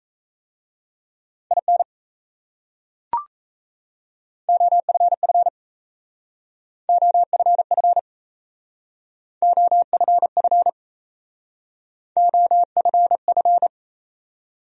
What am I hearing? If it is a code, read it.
Morse code, heard as a keyed tone, about 36 wpm, 700 Hz. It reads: IN OFF OFF OFF OFF